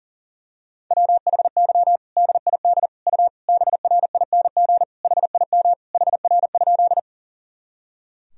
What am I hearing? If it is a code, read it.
Morse code, heard as a keyed tone, about 37 wpm, 700 Hz. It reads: WHY DID U BRING HIM HR?
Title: Sentences from Top 500 Words and Abbreviations 40wpm